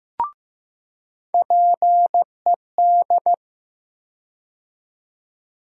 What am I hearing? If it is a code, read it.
Morse code, heard as a keyed tone, about 15 wpm, 700 Hz. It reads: PED
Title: USA Call Sign Postfixes - 3 Letters 15wpm